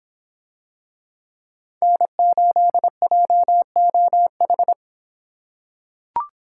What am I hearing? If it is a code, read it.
Morse code, heard as a keyed tone, about 23 wpm, 700 Hz. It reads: N8JOH